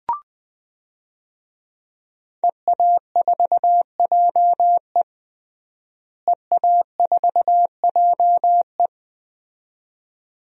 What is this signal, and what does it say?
Morse code, heard as a keyed tone, about 20 wpm, 700 Hz. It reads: EA4JE EA4JE
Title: DXCC-100 Call Signs - 5 Characters - Repeated 2x - 20wpm